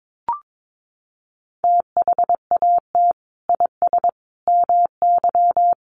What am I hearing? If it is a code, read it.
Morse code, heard as a keyed tone, about 22 wpm, 700 Hz. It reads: THAT IS MY
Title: Sentences from Top 1200 Words 22wpm